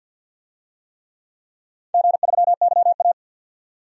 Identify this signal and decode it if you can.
Morse code, heard as a keyed tone, about 52 wpm, 700 Hz. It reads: G3YA
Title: DXCC-100 Call Signs - 4 Characters - 50wpm